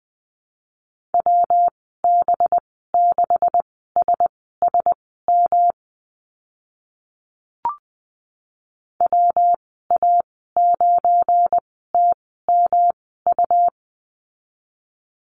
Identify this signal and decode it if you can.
Morse code, heard as a keyed tone, about 19 wpm, spaced as for 14 wpm, 700 Hz. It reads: WB6SSM WA9TMU
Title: 193 Lesson 31g - Letter v - Call Signs 20wpm